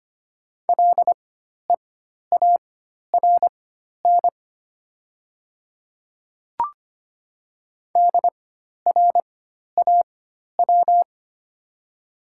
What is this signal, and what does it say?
Morse code, heard as a keyed tone, about 25 wpm, spaced as for 12 wpm, 700 Hz. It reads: LEARN DRAW
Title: 4x Character Spacing - Top 500 Words - 25wpm